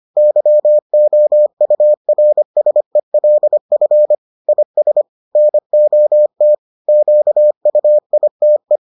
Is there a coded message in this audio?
Yes: Morse code from 0.2 to 8.8 s, heard as a keyed tone, about 25 wpm, 600 Hz. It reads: YOURSELF IS NOT QUITE